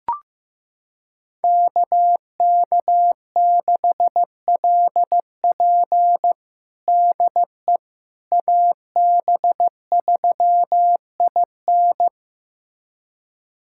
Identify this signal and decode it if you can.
Morse code, heard as a keyed tone, about 16 wpm, 700 Hz. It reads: KK6LP DE AB3IN
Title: USA Callsign DE Callsign - 5 Characters - 15wpm